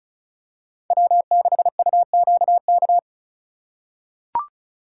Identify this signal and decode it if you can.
Morse code, heard as a keyed tone, about 36 wpm, 700 Hz. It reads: W6UQK
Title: USA Call Signs - 4x Character Spacing - 35wpm